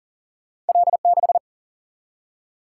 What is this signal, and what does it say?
Morse code, heard as a keyed tone, about 42 wpm, 700 Hz. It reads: L6
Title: Single-Letter Single-Number 40wpm